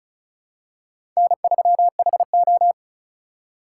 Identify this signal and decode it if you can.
Morse code, heard as a keyed tone, about 34 wpm, 700 Hz. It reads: N3HO
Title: USA Call Signs - 4 Characters - 6x Character Spacing - 35wpm